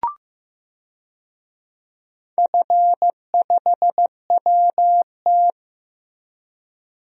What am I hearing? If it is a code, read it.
Morse code, heard as a keyed tone, about 15 wpm, 700 Hz. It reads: F5WT
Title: DXCC-100 Call Signs - 4 Characters - Repeated 2x - 15wpm